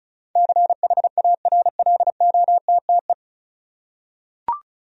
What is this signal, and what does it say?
Morse code, heard as a keyed tone, about 36 wpm, 700 Hz. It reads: CHARLOTTE
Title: USA Common Names - 35wpm